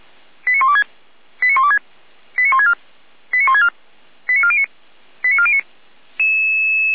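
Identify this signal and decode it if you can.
Five-tone selective call (ZVEI), heard as quick runs of five tones, tones of about 70 ms. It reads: 89126 89126 89164 89164 89409 89409
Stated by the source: ZVEI five-tone selective call 89338 received via SDR.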